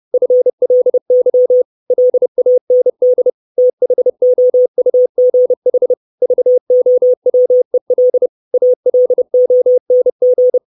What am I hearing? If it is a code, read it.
Morse code, heard as a keyed tone, about 30 wpm, 500 Hz. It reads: FLY LAND THOUGH VOWEL ALONG
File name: 632 Words 30WPM